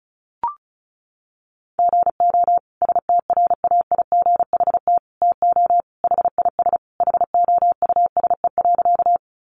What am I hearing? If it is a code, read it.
Morse code, heard as a keyed tone, about 34 wpm, 700 Hz. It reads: GO STRAIGHT TO HIS HOUSE.
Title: Sentences from Top 1000 Words 35wpm